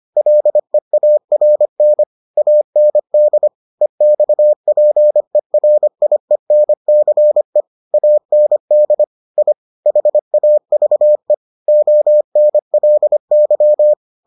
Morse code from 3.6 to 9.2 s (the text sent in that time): EXPERIENCE AND